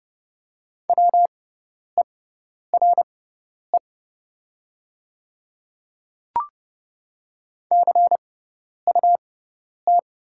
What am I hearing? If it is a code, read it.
Morse code, heard as a keyed tone, about 30 wpm, spaced as for 10 wpm, 700 Hz. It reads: WERE CUT
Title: 6x Character Spacing - Top 300 Words - 30wpm